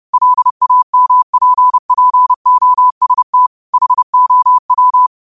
Morse code, heard as a keyed tone, about 30 wpm, 1000 Hz. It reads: LAMPPOST HOW